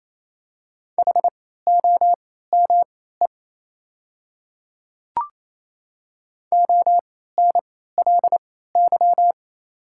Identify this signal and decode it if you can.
Morse code, heard as a keyed tone, about 28 wpm, spaced as for 16 wpm, 700 Hz. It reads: HOME ONLY
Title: 3x Character Spacing - Top 300 Words - 28wpm